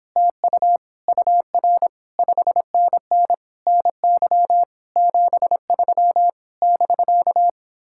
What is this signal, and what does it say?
Morse code, heard as a keyed tone, about 29 wpm, 700 Hz. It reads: TU UR 5NN NY 73 <BK>